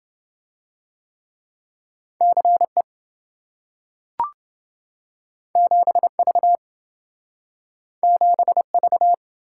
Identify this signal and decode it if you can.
Morse code, heard as a keyed tone, about 27 wpm, 700 Hz. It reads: CE 7V 7V